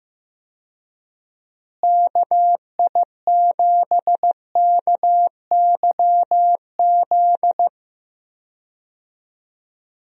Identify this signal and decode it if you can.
Morse code, heard as a keyed tone, about 16 wpm, 700 Hz. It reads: KI7KYZ
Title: USA Call Signs - 6 Characters - Repeated 2x - 15wpm